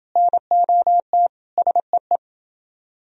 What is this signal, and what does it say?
Morse code, heard as a keyed tone, about 28 wpm, 700 Hz. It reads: NOT SEE